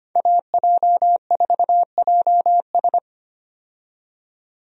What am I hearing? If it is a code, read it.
Morse code, heard as a keyed tone, about 25 wpm, 700 Hz. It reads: AJ4JS
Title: USA Call Signs - 5 Characters - 25wpm